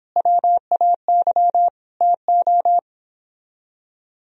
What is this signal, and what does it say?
Morse code, heard as a keyed tone, about 29 wpm, 700 Hz. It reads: WAY TO